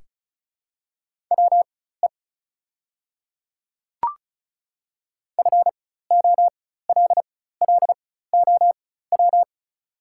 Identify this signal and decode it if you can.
Morse code, heard as a keyed tone, about 36 wpm, spaced as for 16 wpm, 700 Hz. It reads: WE FOLLOW
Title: 4x Character Spacing - Top 200 Words - 35wpm